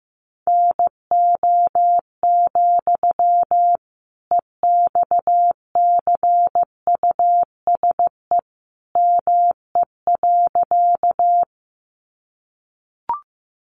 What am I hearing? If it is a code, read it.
Morse code, heard as a keyed tone, about 15 wpm, 700 Hz. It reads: NO, EXCUSE ME.
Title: Sentences from Top 700 Words 15wpm